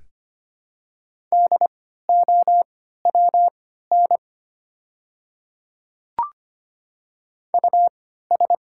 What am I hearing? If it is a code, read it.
Morse code, heard as a keyed tone, about 25 wpm, spaced as for 14 wpm, 700 Hz. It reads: DOWN US
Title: 3x Character Spacing - Top 300 Words - 25wpm